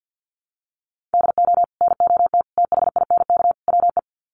Morse code, heard as a keyed tone, about 50 wpm, 700 Hz. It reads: DO NOT THINK WE
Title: Sets of 4 Words 50wpm